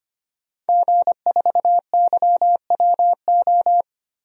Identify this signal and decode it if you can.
Morse code, heard as a keyed tone, about 25 wpm, 700 Hz. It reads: G4YWO
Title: DXCC-100 Call Signs - 5 Characters - Repeated 3x - 25wpm